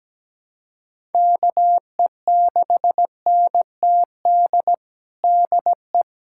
Morse code, heard as a keyed tone, about 17 wpm, 700 Hz. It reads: KE6NTD DE
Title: USA Callsign DE Callsign 17wpm